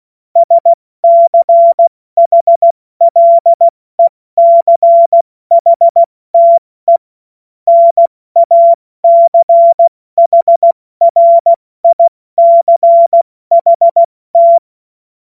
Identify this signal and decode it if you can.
Morse code, heard as a keyed tone, about 16 wpm, 680 Hz. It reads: SCHLECHTE NACHRICHT
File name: Morsekurs / Witze / Witz 13. Juni 2023 70 F80